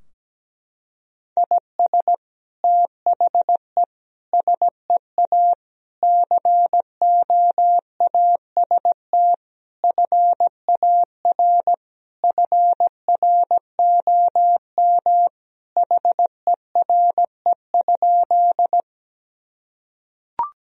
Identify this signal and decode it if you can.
Morse code, heard as a keyed tone, about 17 wpm, 700 Hz. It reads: IS THE SEA COAST FAR FROM HERE?
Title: Sentences from Top 1000 Words - Review Part 4 - 17wpm